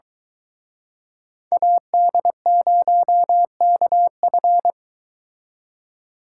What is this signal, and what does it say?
Morse code, heard as a keyed tone, about 22 wpm, 700 Hz. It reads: AD0KF